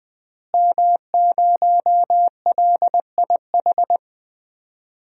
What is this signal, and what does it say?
Morse code, heard as a keyed tone, about 20 wpm, 700 Hz. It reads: M0LIH
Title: DXCC-100 Call Signs - 5 Characters - Repeated 2x - 20wpm